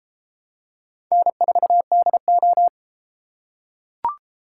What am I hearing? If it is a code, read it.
Morse code, heard as a keyed tone, about 32 wpm, 700 Hz. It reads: N4DO